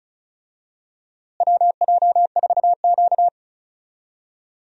Morse code, heard as a keyed tone, about 34 wpm, 700 Hz. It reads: WJ4Q
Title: USA Call Signs - 4 Characters - Repeated 3x - 35wpm